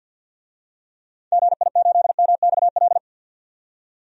Morse code, heard as a keyed tone, about 51 wpm, 700 Hz. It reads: GI8MXL
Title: DXCC-100 Call Signs - 6 Characters - 50wpm